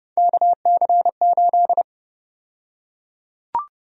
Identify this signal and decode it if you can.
Morse code, heard as a keyed tone, about 30 wpm, 700 Hz. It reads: KC8
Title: USA Call Sign Prefixes - Repeated 2x - 30wpm